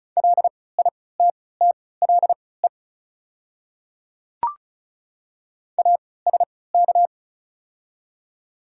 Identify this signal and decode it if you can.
Morse code, heard as a keyed tone, about 36 wpm, spaced as for 20 wpm, 700 Hz. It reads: LITTLE ASK
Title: 3x Character Spacing - Top 200 Words - 35wpm